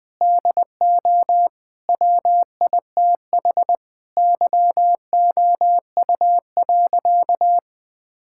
Morse code, heard as a keyed tone, about 20 wpm, 700 Hz. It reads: DO WITH YOU.
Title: Sentences with 3-Word Phrases - 20wpm